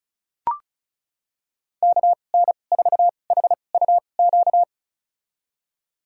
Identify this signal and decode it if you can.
Morse code, heard as a keyed tone, about 36 wpm, spaced as for 25 wpm, 700 Hz. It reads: KN4HUQ
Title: USA Call Signs - 6 Characters - 2x Character Spacing - 35wpm